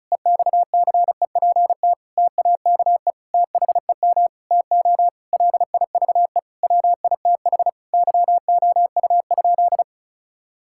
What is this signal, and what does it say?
Morse code, heard as a keyed tone, about 34 wpm, 700 Hz. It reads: EXCEPT TAKE THEM TO LIVE WITH YOU?
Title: Sentences from Top 1600 Words 35wpm